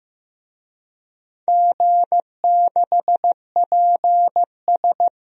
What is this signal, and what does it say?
Morse code, heard as a keyed tone, about 15 wpm, 700 Hz. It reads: G6PS
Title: DXCC-100 Call Signs - 4 Characters - Repeated 2x - 15wpm